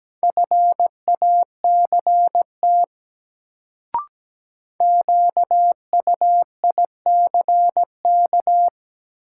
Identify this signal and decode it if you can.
Morse code, heard as a keyed tone, about 17 wpm, 700 Hz. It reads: FACT QUICK
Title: Sentences from Top 500 Words - New Word Drill - 17wpm